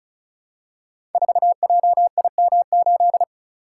Morse code, heard as a keyed tone, about 36 wpm, 690 Hz. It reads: 4JIM8
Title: Code Groups - 5 Letter:Numbers - 35wpm